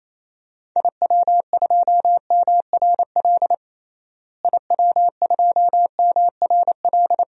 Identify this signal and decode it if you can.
Morse code, heard as a keyed tone, about 28 wpm, 700 Hz. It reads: IW2MRL IW2MRL
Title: DXCC-100 Call Signs - 6 Characters - Repeated 3x - 28wpm